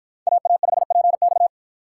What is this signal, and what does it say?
Morse code, heard as a keyed tone, about 47 wpm, 700 Hz. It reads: AA5PX